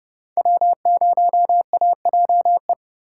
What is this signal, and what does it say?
Morse code, heard as a keyed tone, about 38 wpm, 700 Hz. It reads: W0AJE
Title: Speed Racing - USA 1x3 Call Signs 30wpm